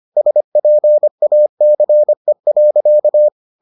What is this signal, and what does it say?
Morse code, heard as a keyed tone, about 25 wpm, 600 Hz. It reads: SPACE.